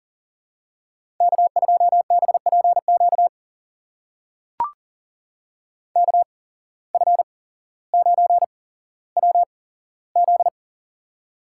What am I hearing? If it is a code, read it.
Morse code, heard as a keyed tone, about 38 wpm, spaced as for 17 wpm, 700 Hz. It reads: K2BPQ KF9WZ
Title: USA Call Signs - 5 Characters - 8x Character Spacing - 40wpm